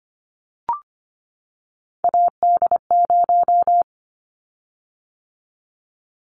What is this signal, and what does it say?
Morse code, heard as a keyed tone, about 25 wpm, 700 Hz. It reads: AD0
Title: Callsign Prefixes for Top 25 DXCC - 25wpm